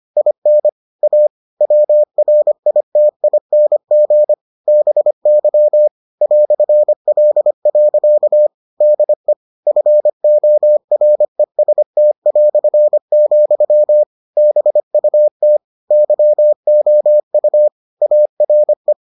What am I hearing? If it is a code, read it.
Morse code, heard as a keyed tone, about 25 wpm, 600 Hz. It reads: IN A WRITING BY "L. DE FOREST", BUT YOU ARE